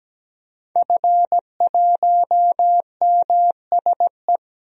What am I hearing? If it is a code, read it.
Morse code, heard as a keyed tone, about 17 wpm, 700 Hz. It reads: F1MSE
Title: DXCC-100 Call Signs - 5 Characters - 17wpm